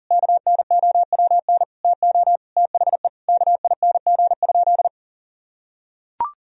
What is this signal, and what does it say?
Morse code, heard as a keyed tone, about 40 wpm, 700 Hz. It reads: KNOWN TO THE KING?
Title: Sentences from Top 1000 Words - Review Part 4 - 40wpm